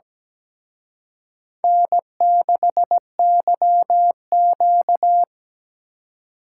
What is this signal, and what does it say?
Morse code, heard as a keyed tone, about 20 wpm, 700 Hz. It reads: N6YQ
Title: Speed Racing - USA 1x2 Call Signs 17wpm